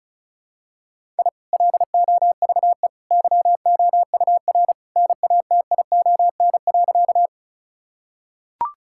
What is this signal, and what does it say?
Morse code, heard as a keyed tone, about 36 wpm, 700 Hz. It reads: I LOVE YOUR NATION.
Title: Sentences from Top 1000 Words - Review Part 6 - 35wpm